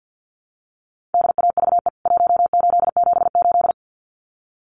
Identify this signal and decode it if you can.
Morse code, heard as a keyed tone, about 49 wpm, 700 Hz. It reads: DAVE 1878
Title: CWT Exchange 50wpm